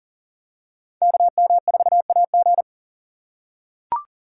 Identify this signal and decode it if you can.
Morse code, heard as a keyed tone, about 38 wpm, 700 Hz. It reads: KM4AG
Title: USA Call Signs - 5 Characters - 4x Character Spacing - 40wpm